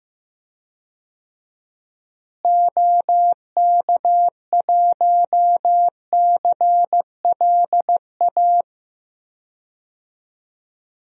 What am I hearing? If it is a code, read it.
Morse code, heard as a keyed tone, about 15 wpm, 700 Hz. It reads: OK1CLA
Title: DXCC-100 Call Signs - 6 Characters - Repeated 3x - 15wpm